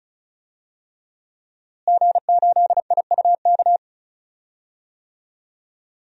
Morse code, heard as a keyed tone, about 34 wpm, 700 Hz. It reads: G8IUK
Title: DXCC-100 Call Signs - 5 Characters - Repeated 3x - 35wpm